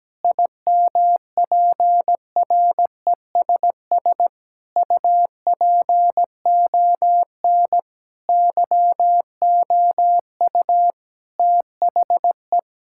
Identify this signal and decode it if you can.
Morse code, heard as a keyed tone, about 17 wpm, 700 Hz. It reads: IMPRESS UPON YOU THE
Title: Sentences from Top 1600 Words 17wpm